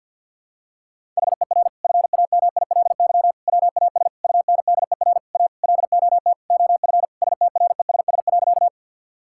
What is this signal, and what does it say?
Morse code, heard as a keyed tone, about 52 wpm, 700 Hz. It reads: HER FAMILY WAS UNDER A LOT OF STRESS.
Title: Sentences from Top 1000 Words - Review Part 6 - 50wpm